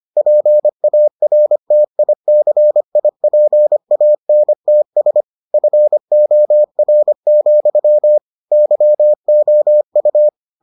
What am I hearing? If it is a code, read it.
Morse code, heard as a keyed tone, about 25 wpm, 600 Hz. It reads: PARTICIPANTS FOR, YOU